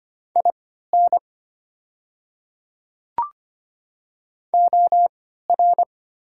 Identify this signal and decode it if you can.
Morse code, heard as a keyed tone, about 25 wpm, spaced as for 14 wpm, 700 Hz. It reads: IN OR
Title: 3x Character Spacing - Top 100 Words - 25wpm